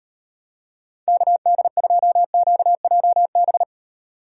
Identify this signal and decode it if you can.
Morse code, heard as a keyed tone, about 35 wpm, 700 Hz. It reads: KD2QJB